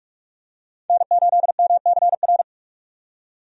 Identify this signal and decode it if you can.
Morse code, heard as a keyed tone, about 46 wpm, 700 Hz. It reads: N8MCR
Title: USA Call Signs - 5 Characters - 45wpm